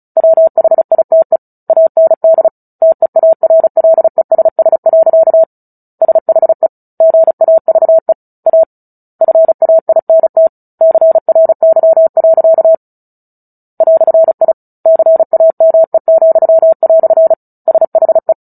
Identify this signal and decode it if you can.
Morse code, heard as a keyed tone, about 34 wpm, 650 Hz. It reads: WHITE AND TEARLESS. SHE GAVE A FAINT CRY. "I CAME," SHE
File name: Book Two / 09-Chapter 9 - 35 WPM